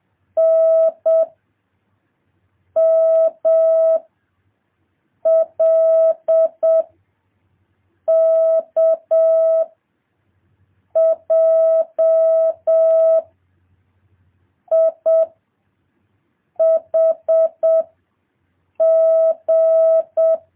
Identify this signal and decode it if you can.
Morse code, heard as a keyed tone, about 7 wpm, 650 Hz. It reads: N M L K J I H G
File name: jh8cbh / 2023cw